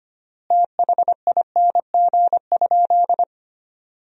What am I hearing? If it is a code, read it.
Morse code, heard as a keyed tone, about 25 wpm, 700 Hz. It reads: THING?
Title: 3x Word-Spacing - Sentences from Top 200 Words - 25wpm